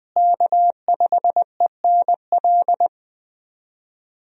Morse code, heard as a keyed tone, about 20 wpm, 700 Hz. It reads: K5ENL
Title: USA Call Signs - 5 Characters - Repeated 3x - 20wpm